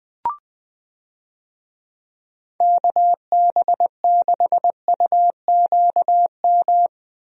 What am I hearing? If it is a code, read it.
Morse code, heard as a keyed tone, about 20 wpm, 700 Hz. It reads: KB6UQM
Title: USA Call Signs - 6 Characters - Repeated 2x - 20wpm